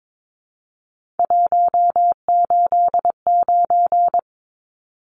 Morse code, heard as a keyed tone, about 22 wpm, 700 Hz. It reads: 189
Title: Numbers - 3-Digit 22wpm